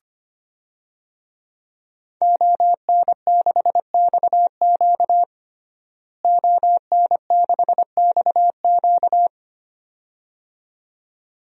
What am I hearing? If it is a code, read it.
Morse code, heard as a keyed tone, about 25 wpm, 700 Hz. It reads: ON6XQ ON6XQ
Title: DXCC-100 Call Signs - 5 Characters - Repeated 2x - 25wpm